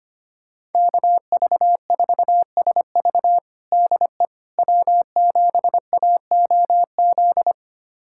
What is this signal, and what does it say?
Morse code, heard as a keyed tone, about 25 wpm, 700 Hz. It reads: KV4SV DE W7AOZ
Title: USA Callsign DE Callsign - 5 Characters - 25wpm